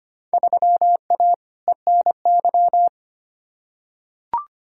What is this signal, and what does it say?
Morse code, heard as a keyed tone, about 25 wpm, 700 Hz. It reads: 3A ENY
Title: Field Day Exchanges - 25wpm